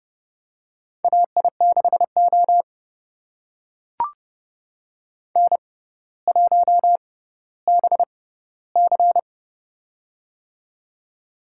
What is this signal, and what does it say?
Morse code, heard as a keyed tone, about 29 wpm, spaced as for 15 wpm, 700 Hz. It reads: AI6O N1BC
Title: USA Call Signs - 4 Characters - 6x Character Spacing - 30wpm